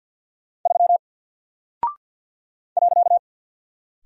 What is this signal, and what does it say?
Morse code, heard as a keyed tone, about 49 wpm, 700 Hz. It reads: 3 .
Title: Single Character (Letter:Number:Punctuation:Prosign) Alt 50wpm